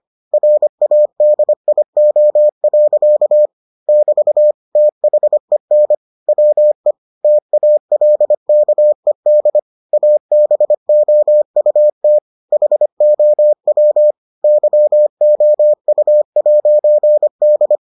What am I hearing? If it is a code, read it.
Morse code, heard as a keyed tone, about 25 wpm, 600 Hz. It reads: RADIO. <BT> THEN WE TALKED ABOUT HOW YOU'D